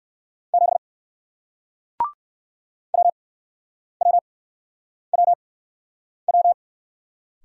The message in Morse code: 7 WWWW

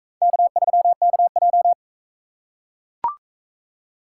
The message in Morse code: K3KJ

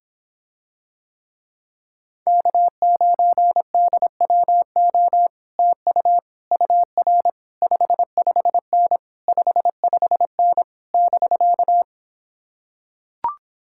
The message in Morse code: K9DWO TU UR 55N 55N <BK>

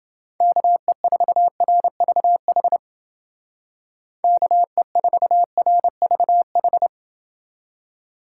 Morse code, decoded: KE4RVH KE4RVH

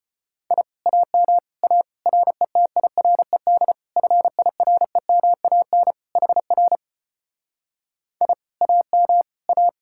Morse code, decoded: I AM A RETIRED FIREMAN HR I AM A

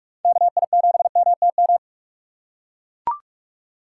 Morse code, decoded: KI7MTM

